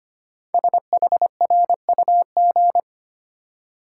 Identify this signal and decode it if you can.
Morse code, heard as a keyed tone, about 25 wpm, 700 Hz. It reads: SHRUG
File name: 5 Letter Words - First 3 Letters Given - 25wpm